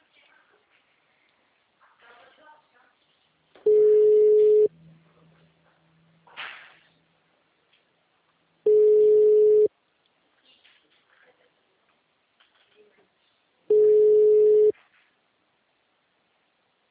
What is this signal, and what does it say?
Ringback tone (Europe, 425 Hz), with speech.